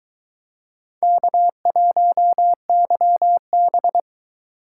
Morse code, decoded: K1YB

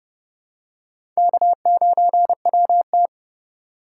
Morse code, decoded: K9WT

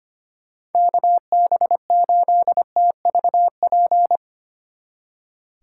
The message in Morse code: KB8TVP

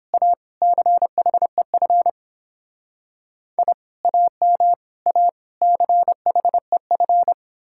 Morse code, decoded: A CHEF I AM A CHEF

